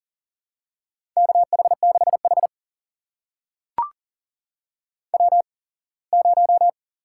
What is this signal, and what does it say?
Morse code, heard as a keyed tone, about 40 wpm, spaced as for 24 wpm, 700 Hz. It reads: KH6H W0
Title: USA Call Signs - 4 Characters - 8x Character Spacing - 40wpm